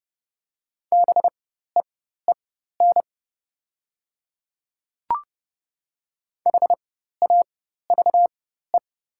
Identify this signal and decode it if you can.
Morse code, heard as a keyed tone, about 30 wpm, spaced as for 14 wpm, 700 Hz. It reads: BEEN HAVE
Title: 4x Character Spacing - Top 100 Words - 30wpm